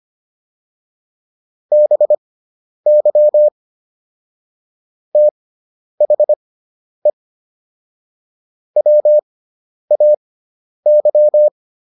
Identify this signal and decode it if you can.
Morse code, heard as a keyed tone, about 25 wpm, spaced as for 10 wpm, 600 Hz. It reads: BY THE WAY